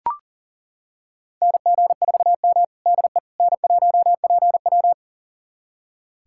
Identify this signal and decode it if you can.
Morse code, heard as a keyed tone, about 40 wpm, 700 Hz. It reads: NG4M DE N1PW
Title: USA Callsign DE Callsign - 4 Characters - 40wpm